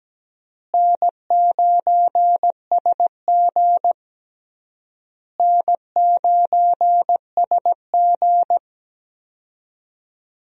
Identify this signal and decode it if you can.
Morse code, heard as a keyed tone, about 17 wpm, 700 Hz. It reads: N9SG N9SG
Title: USA Call Signs - 4 Characters - Repeated 3x - 17wpm